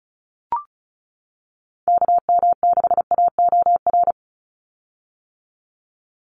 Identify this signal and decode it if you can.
Morse code, heard as a keyed tone, about 36 wpm, 700 Hz. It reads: KM6AOR
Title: USA Call Signs 35wpm